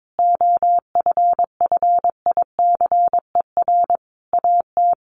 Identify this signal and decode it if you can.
Morse code, heard as a keyed tone, about 22 wpm, 700 Hz. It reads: OFFICER AT
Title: Sentences from Top 1300 Words 22wpm